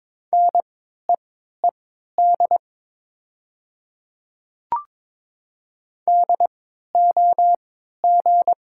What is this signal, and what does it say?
Morse code, heard as a keyed tone, about 22 wpm, spaced as for 12 wpm, 700 Hz. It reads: NEED DOG
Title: 3x Character Spacing - Top 500 Words - 22wpm